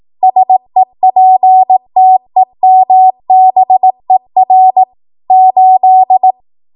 Morse code, heard as a keyed tone, about 18 wpm, 750 Hz. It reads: SEPTEMBER 8
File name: Archive / 18 WPM / 18WPM